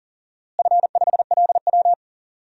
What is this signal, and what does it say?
Morse code, heard as a keyed tone, about 40 wpm, 700 Hz. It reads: F5LW
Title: DXCC-100 Call Signs - 4 Characters - Repeated 3x - 40wpm